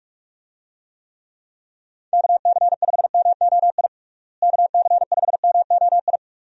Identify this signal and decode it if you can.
Morse code, heard as a keyed tone, about 46 wpm, 700 Hz. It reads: KC5MOI KC5MOI
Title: USA Call Signs - 6 Characters - Repeated 3x - 45wpm